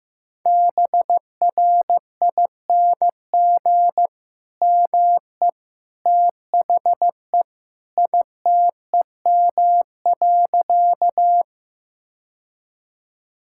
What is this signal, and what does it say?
Morse code, heard as a keyed tone, about 15 wpm, 700 Hz. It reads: BRING ME THE ITEM.
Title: Sentences from Top 1000 Words - Review Part 6 - 15wpm